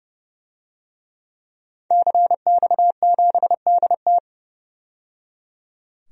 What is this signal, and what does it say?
Morse code, heard as a keyed tone, about 31 wpm, 700 Hz. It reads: CX7DT
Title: DXCC-100 Call Signs - 5 Characters - 30wpm